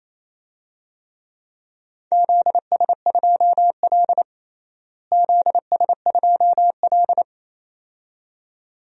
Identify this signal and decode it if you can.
Morse code, heard as a keyed tone, about 28 wpm, 700 Hz. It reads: ZS2L ZS2L
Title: DXCC-100 Call Signs - 4 Characters - Repeated 2x - 28wpm